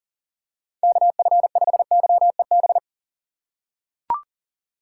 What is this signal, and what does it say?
Morse code, heard as a keyed tone, about 40 wpm, 700 Hz. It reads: KF5YEB
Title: USA Call Signs - 6 Characters - 6x Character Spacing - 40wpm